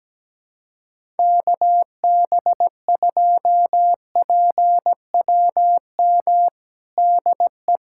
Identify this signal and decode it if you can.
Morse code, heard as a keyed tone, about 17 wpm, 700 Hz. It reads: KB2PWM DE
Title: USA Callsign DE Callsign - 6 Characters - 17wpm